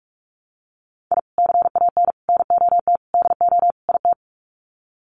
Morse code, heard as a keyed tone, about 44 wpm, 700 Hz. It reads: I CAN NOT DO IT
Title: Sets of 5 Words Encore 45wpm